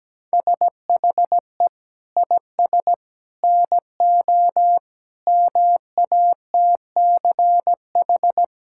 Morse code, heard as a keyed tone, about 17 wpm, 700 Hz. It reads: SHE IS NO MATCH
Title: Sentences from Top 1600 Words 17wpm